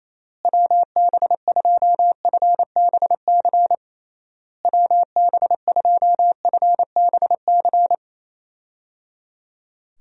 Morse code, heard as a keyed tone, about 28 wpm, 700 Hz. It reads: WB2FBC WB2FBC
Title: USA Call Signs - 6 Characters - Repeated 3x - 28wpm